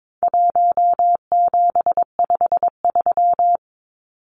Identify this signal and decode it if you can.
Morse code, heard as a keyed tone, about 22 wpm, 700 Hz. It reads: 1753
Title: Numbers - 4-Digit 22wpm